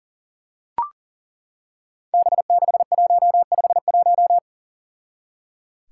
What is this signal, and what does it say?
Morse code, heard as a keyed tone, about 37 wpm, 700 Hz. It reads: B6151
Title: Five Challenging Letter-Number Combinations 40wpm